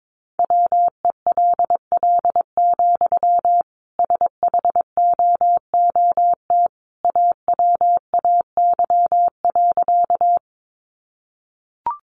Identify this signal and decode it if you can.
Morse code, heard as a keyed tone, about 22 wpm, 700 Hz. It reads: WELL, SHOOT AWAY.
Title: Sentences from Top 1200 Words 22wpm